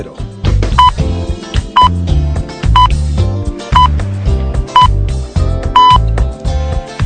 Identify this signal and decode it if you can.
Radio time signal (time pips). It is heard from 0.8 to 6.0 s.